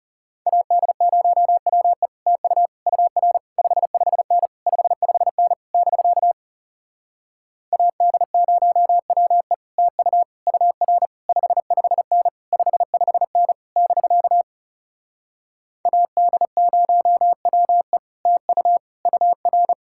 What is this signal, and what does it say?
Morse code, heard as a keyed tone, about 40 wpm, 700 Hz. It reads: AD0WE TU UR 55N 55N <BK> AD0WE TU UR 55N 55N <BK> AD0WE TU UR